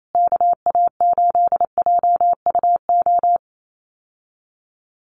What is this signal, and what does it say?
Morse code, heard as a keyed tone, about 28 wpm, 700 Hz. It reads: KA8JUO
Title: USA Call Signs Repeated 2x 28wpm